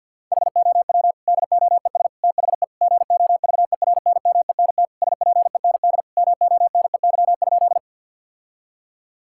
Morse code, heard as a keyed tone, about 51 wpm, 700 Hz. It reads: HOW DOES THE GOVERNMENT SPEND MONEY?
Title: Sentences from Top 1000 Words - Review Part 5 - 50wpm